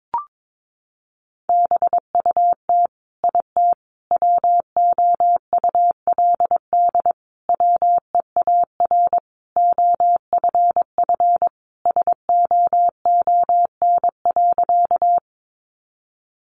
Morse code, heard as a keyed tone, about 22 wpm, 700 Hz. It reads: BUT IT WOULD WEAR OFF SOON.
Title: Sentences from Top 1500 Words 22wpm